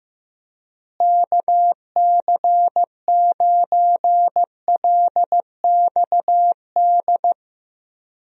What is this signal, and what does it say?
Morse code, heard as a keyed tone, about 15 wpm, 700 Hz. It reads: KC9LXD
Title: USA Call Signs - 6 Characters - Repeated 3x - 15wpm